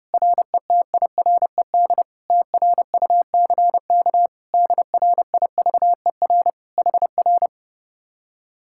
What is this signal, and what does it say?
Morse code, heard as a keyed tone, about 30 wpm, 700 Hz. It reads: RETIRED TRUCK DRIVER HR